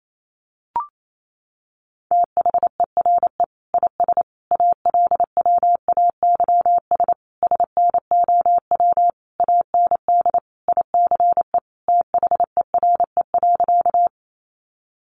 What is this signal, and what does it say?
Morse code, heard as a keyed tone, about 28 wpm, 700 Hz. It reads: THERE IS ALWAYS SNOW AND ICE THERE.
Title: Sentences from Top 1000 Words 28wpm